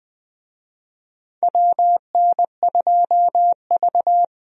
Morse code, heard as a keyed tone, about 20 wpm, 700 Hz. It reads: WN2V
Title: USA Call Signs - 4 Characters - Repeated 3x - 20wpm